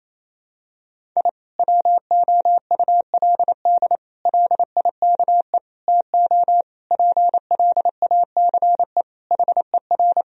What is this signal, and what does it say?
Morse code, heard as a keyed tone, about 28 wpm, 700 Hz. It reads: I WOULD LIKE TO PLACE HER
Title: Sentences from Top 1000 Words - Review Part 1 - 28wpm